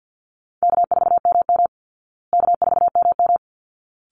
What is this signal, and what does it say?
Morse code, heard as a keyed tone, about 48 wpm, 700 Hz. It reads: K4MM K4MM
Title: Easy Call Signs Repeated 2x - 50wpm